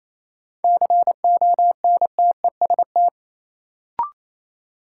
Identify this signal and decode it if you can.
Morse code, heard as a keyed tone, about 28 wpm, 700 Hz. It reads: CONTEST
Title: Sentences from Top 1600 Words - New Word Drill - 28wpm